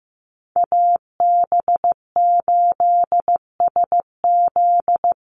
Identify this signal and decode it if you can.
Morse code, heard as a keyed tone, about 15 wpm, 700 Hz. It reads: AB8SZ
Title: USA Call Signs 15wpm